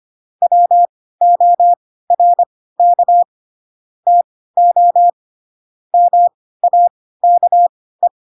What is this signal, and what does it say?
Morse code, heard as a keyed tone, about 25 wpm, spaced as for 16 wpm, 700 Hz. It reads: WORK TO MAKE
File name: Pinocchio / Pinocchio ch3 10 25 ik1hge